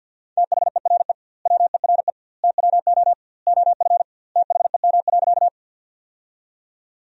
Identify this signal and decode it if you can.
Morse code, heard as a keyed tone, about 50 wpm, 700 Hz. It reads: THERE WERE TWO OF THEM.
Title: 2x Word-Spacing - Sentences from Top 100 Words - 50wpm